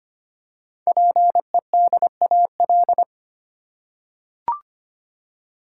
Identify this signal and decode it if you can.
Morse code, heard as a keyed tone, about 25 wpm, 700 Hz. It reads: PEDAL
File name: practice / 25wpm / Easy Words with Hints - 25wpm